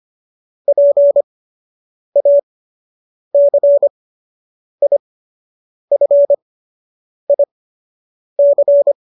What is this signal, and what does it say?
Morse code, heard as a keyed tone, about 25 wpm, spaced as for 8 wpm, 570 Hz. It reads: PACIFIC